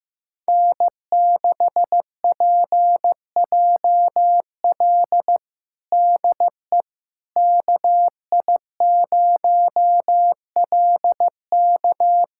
Morse code, heard as a keyed tone, about 16 wpm, 700 Hz. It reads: N6PJL DE KI0LK
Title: USA Callsign DE Callsign - 5 Characters - 15wpm